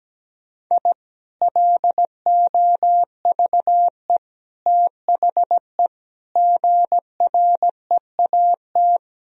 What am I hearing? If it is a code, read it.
Morse code, heard as a keyed tone, about 17 wpm, 700 Hz. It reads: I LOVE THE GREAT
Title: Sentences from Top 1000 Words - Review Part 1 - 17wpm